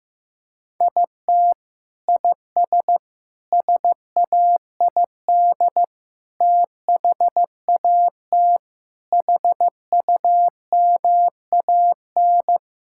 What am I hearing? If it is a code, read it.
Morse code, heard as a keyed tone, about 16 wpm, 700 Hz. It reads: IT IS SAID THAT HUMAN
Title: Sentences from Top 1000 Words - Review Part 6 - 15wpm